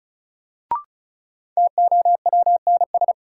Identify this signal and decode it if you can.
Morse code, heard as a keyed tone, about 34 wpm, 700 Hz. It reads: TOWNS
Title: Sentences from Top 1000 Words - Review Part 6 - New Word Drill - 35wpm